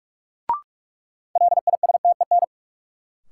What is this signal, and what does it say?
Morse code, heard as a keyed tone, about 43 wpm, 700 Hz. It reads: LISTEN
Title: Sentences from Top 400 Words - New Word Drill - 45wpm